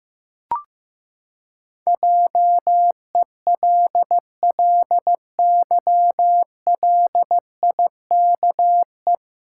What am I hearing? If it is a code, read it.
Morse code, heard as a keyed tone, about 15 wpm, 700 Hz. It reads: JELLYLIKE
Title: Pangram Challenge v2 - 15wpm